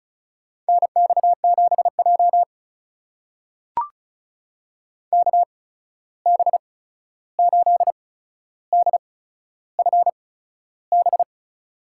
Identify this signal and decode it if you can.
Morse code, heard as a keyed tone, about 36 wpm, spaced as for 13 wpm, 700 Hz. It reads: NX7J KB8DFB
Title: USA Call Signs - 8x Character Spacing - 35wpm